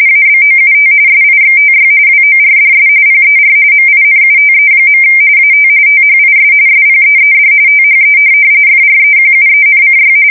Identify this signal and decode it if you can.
RTTY, about 45.45 baud. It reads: ABOUT SIXTY WORDS PER MINUTE. AT THIS SPEED THE FREQUENCY
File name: fsk170 / printing-telegraph